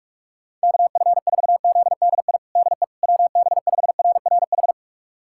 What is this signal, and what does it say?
Morse code, heard as a keyed tone, about 46 wpm, 700 Hz. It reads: KF4ZDI DE WB5RRH
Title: USA Callsign DE Callsign - 6 Characters - 45wpm